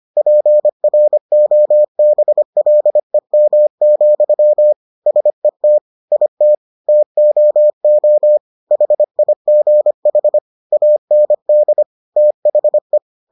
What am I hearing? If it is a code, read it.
Morse code, heard as a keyed tone, about 25 wpm, 600 Hz. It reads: PROBLEM, SET IT TOO HIGH AND THE